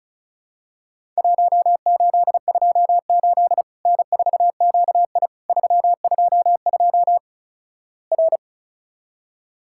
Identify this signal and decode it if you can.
Morse code, heard as a keyed tone, about 36 wpm, 700 Hz. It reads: 1828 N4QI 322 R
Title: Worked All Europe DX Contest - QTC Exchange - 35wpm